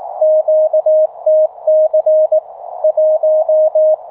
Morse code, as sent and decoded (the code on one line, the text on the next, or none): --.- - -.-. / .----
QTC 1